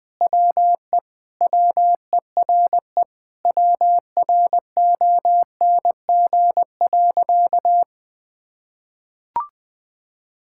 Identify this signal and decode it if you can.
Morse code, heard as a keyed tone, about 20 wpm, 700 Hz. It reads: WE WERE WRONG.
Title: Sentences with 5-Word Phrases - 20wpm